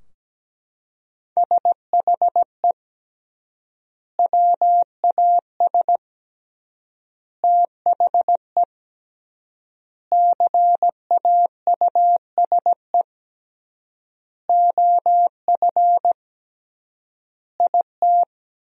Morse code, heard as a keyed tone, about 17 wpm, 700 Hz. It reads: SHE WAS THE CAUSE OF IT
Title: 3x Word-Spacing - Sentences from Top 200 Words - 17wpm